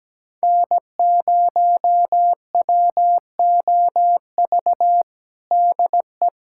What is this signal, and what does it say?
Morse code, heard as a keyed tone, about 17 wpm, 700 Hz. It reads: N0WOV DE
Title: USA Callsign DE Callsign - 5 Characters - 17wpm